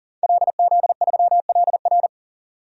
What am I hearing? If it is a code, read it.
Morse code, heard as a keyed tone, about 40 wpm, 700 Hz. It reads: LZ3LR
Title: DXCC-100 Call Signs - 5 Characters - 40wpm